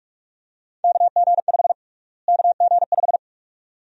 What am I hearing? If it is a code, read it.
Morse code, heard as a keyed tone, about 46 wpm, 700 Hz. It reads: KG5 KG5
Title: USA Call Sign Prefixes - Repeated 2x - 45wpm